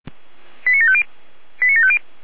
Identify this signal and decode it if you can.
Five-tone selective call (ZVEI), heard as quick runs of five tones, tones of about 70 ms. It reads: 89750 89750